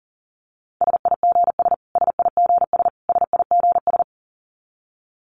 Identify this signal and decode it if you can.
Morse code, heard as a keyed tone, about 40 wpm, 700 Hz. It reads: SIGS SIGS SIGS
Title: Abbreviations Repeated 3x - 40wpm